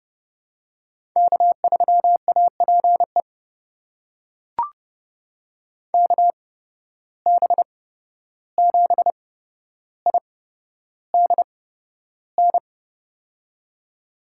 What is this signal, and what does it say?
Morse code, heard as a keyed tone, about 30 wpm, spaced as for 12 wpm, 700 Hz. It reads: K3APE KB7IDN
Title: USA Call Signs - 8x Character Spacing - 30wpm